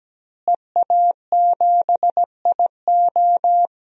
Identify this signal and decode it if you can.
Morse code, heard as a keyed tone, about 17 wpm, 700 Hz. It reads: EA7IO
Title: DXCC-100 Call Signs - 5 Characters - Repeated 3x - 17wpm